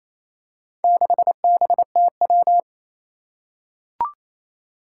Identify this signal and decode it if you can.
Morse code, heard as a keyed tone, about 28 wpm, 700 Hz. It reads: 6BTW
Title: Four Challenging Letter-Number Combinations 28wpm